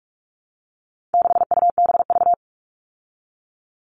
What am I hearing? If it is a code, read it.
Morse code, heard as a keyed tone, about 44 wpm, 700 Hz. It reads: 6UBV
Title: All Cut Numbers - 4-Digit 45wpm